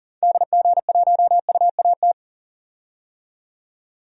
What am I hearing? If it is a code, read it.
Morse code, heard as a keyed tone, about 40 wpm, 700 Hz. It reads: DG1UAT
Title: DXCC-100 Call Signs - 6 Characters - 40wpm